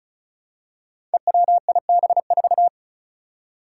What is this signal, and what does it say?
Morse code, heard as a keyed tone, about 36 wpm, 700 Hz. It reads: EWIB4
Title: Five Challenging Letter-Number Combinations 35wpm